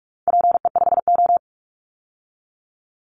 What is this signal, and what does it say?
Morse code, heard as a keyed tone, about 44 wpm, 700 Hz. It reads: PE5O
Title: Call Signs Repeated 3x - 45wpm